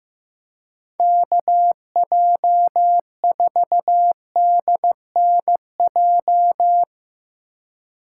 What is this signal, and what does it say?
Morse code, heard as a keyed tone, about 17 wpm, 700 Hz. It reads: KJ4DNJ
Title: Speed Racing - USA 2x3 Call Signs 15wpm